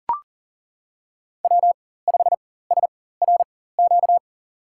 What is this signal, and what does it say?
Morse code, heard as a keyed tone, about 40 wpm, spaced as for 19 wpm, 700 Hz. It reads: W5SRQ
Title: USA Call Signs - 5 Characters - 4x Character Spacing - 40wpm